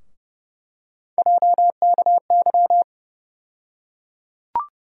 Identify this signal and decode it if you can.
Morse code, heard as a keyed tone, about 30 wpm, 700 Hz. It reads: JKY
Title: USA Call Sign Postfixes - 3 Letters 30wpm